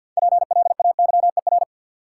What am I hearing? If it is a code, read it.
Morse code, heard as a keyed tone, about 51 wpm, 700 Hz. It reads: PLAYER
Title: Sentences from Top 1000 Words - Review Part 4 - New Word Drill - 50wpm